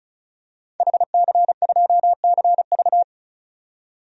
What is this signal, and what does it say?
Morse code, heard as a keyed tone, about 34 wpm, 700 Hz. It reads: HC2CV
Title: DXCC-100 Call Signs - 5 Characters - 35wpm